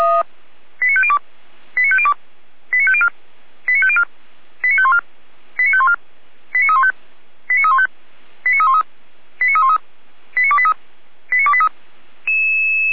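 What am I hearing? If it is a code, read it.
Five-tone selective call (ZVEI), heard as quick runs of five tones, tones of about 70 ms. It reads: 89502 89502 89504 89504 89514 89514 89315 89315 89313 89313 89283 89283
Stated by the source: ZVEI five-tone selective call received by SDR.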